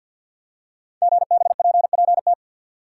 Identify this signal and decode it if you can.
Morse code, heard as a keyed tone, about 50 wpm, 700 Hz. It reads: GBPPT
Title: Code Groups - 5 Letters - 50wpm